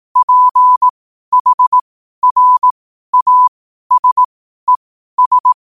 Morse code, heard as a keyed tone, about 18 wpm, spaced as for 12 wpm, 1000 Hz. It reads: PHRASES
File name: qotdfiles / 12wpm